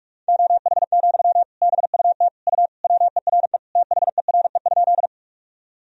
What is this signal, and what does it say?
Morse code, heard as a keyed tone, about 46 wpm, 700 Hz. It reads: OH, BUT U WERE THERE?
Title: Sentences from Top 500 Words and Abbreviations 45wpm